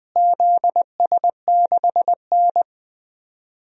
Morse code, heard as a keyed tone, about 20 wpm, 700 Hz. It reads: ZS6N
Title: DXCC-100 Call Signs - 4 Characters - Repeated 3x - 20wpm